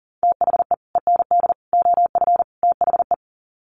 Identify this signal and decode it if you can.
Morse code, heard as a keyed tone, about 40 wpm, 700 Hz. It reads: THE END OF THE
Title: Sets of 4 Words 40wpm